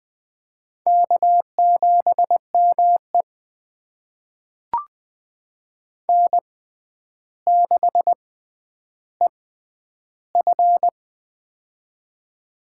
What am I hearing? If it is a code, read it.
Morse code, heard as a keyed tone, about 20 wpm, spaced as for 10 wpm, 700 Hz. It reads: K7ME N6EF
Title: USA Call Signs - 4 Characters - 6x Character Spacing - 20wpm